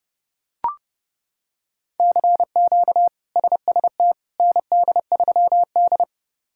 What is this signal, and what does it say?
Morse code, heard as a keyed tone, about 29 wpm, 700 Hz. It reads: CQ SST ND3D
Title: K1USN SST QSO Practice - 30wpm